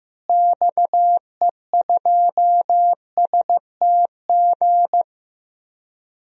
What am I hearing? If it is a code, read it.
Morse code, heard as a keyed tone, about 15 wpm, 700 Hz. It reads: XE2STG
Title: DXCC-100 Call Signs - 6 Characters - Repeated 2x - 15wpm